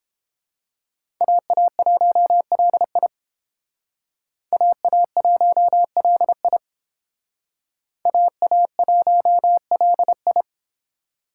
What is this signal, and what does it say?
Morse code, heard as a keyed tone, about 30 wpm, 700 Hz. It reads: AA1LI AA1LI AA1LI